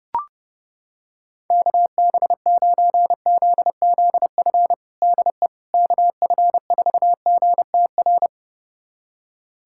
Morse code, heard as a keyed tone, about 32 wpm, 700 Hz. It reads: KB9ZZF DE KF4GTR
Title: USA Callsign DE Callsign - 6 Characters - 30wpm